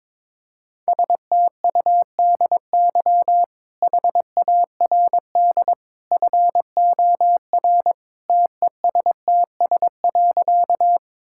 Morse code, heard as a keyed tone, about 22 wpm, 700 Hz. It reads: STUDY HARD FOR TESTS.
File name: practice / 22wpm / Easy Sentences with Hints - 22wpm